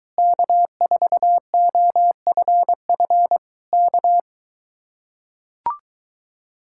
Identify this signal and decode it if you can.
Morse code, heard as a keyed tone, about 23 wpm, 700 Hz. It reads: K4OFF K